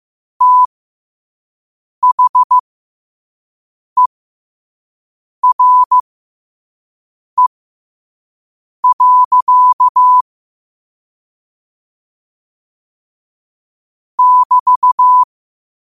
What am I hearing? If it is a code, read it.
Morse code, heard as a keyed tone, about 15 wpm, spaced as for 5 wpm, 1000 Hz. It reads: THERE. <BT>